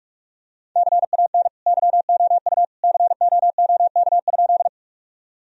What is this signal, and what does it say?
Morse code, heard as a keyed tone, about 46 wpm, 700 Hz. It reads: CAN YOU COOK?
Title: Sentences from Top 1000 Words - Review Part 4 - 45wpm